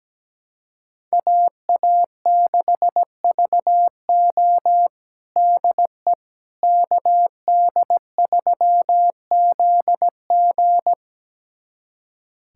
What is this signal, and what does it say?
Morse code, heard as a keyed tone, about 17 wpm, 700 Hz. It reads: AA6VO DE KD3ZG
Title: USA Callsign DE Callsign - 5 Characters - 17wpm